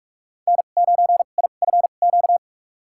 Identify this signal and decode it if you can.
Morse code, heard as a keyed tone, about 42 wpm, spaced as for 31 wpm, 700 Hz. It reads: N9IFQ
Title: USA Call Signs - 2x Character Spacing - 45wpm